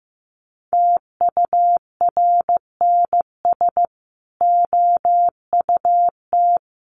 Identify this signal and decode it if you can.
Morse code, heard as a keyed tone, about 15 wpm, 700 Hz. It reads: TURNS OUT
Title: Sets of 5 Words Encore 15wpm